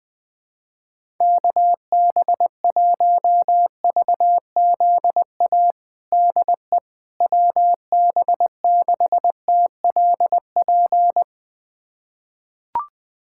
Morse code, heard as a keyed tone, about 20 wpm, 700 Hz. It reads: KB1VZA DE WB6TLP
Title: USA Callsign DE Callsign - 6 Characters - 20wpm